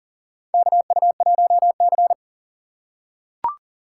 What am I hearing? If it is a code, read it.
Morse code, heard as a keyed tone, about 40 wpm, 700 Hz. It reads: KU1C
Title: USA Call Signs - 4x Character Spacing - 40wpm